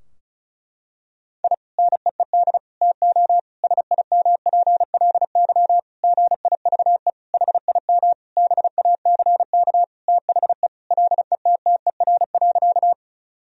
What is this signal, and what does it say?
Morse code, heard as a keyed tone, about 34 wpm, 700 Hz. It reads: I NEED TO SIMPLY GIVE HIM BACK THE LETTER.
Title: Sentences from Top 1000 Words - Review Part 4 - 35wpm